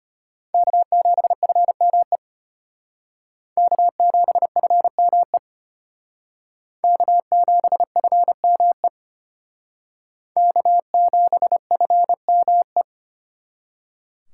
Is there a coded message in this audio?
Yes: Morse code from 0.5 to 12.8 s, heard as a keyed tone, about 32 wpm, 700 Hz. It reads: K7FME K7FME K7FME K7FME